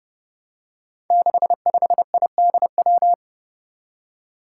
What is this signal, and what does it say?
Morse code, heard as a keyed tone, about 30 wpm, 700 Hz. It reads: 65IDW
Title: Five Challenging Letter-Number Combinations 30wpm